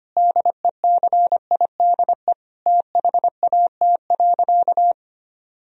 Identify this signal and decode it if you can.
Morse code, heard as a keyed tone, about 25 wpm, 700 Hz. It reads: DECIDE THAT.
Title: Sentences from Top 1000 Words - Review Part 3 - 25wpm